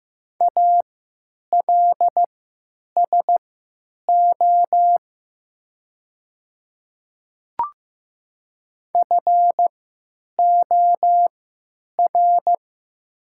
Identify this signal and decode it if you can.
Morse code, heard as a keyed tone, about 15 wpm, spaced as for 9 wpm, 700 Hz. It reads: ALSO FOR
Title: 3x Character Spacing - Top 300 Words - 15wpm